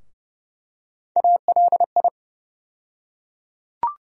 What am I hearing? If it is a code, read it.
Morse code, heard as a keyed tone, about 30 wpm, 700 Hz. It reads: ALI
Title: USA Call Sign Postfixes - 3 Letters 30wpm